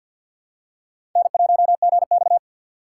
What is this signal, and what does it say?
Morse code, heard as a keyed tone, about 49 wpm, 700 Hz. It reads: N1GX
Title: USA Call Signs - 4 Characters - 6x Character Spacing - 50wpm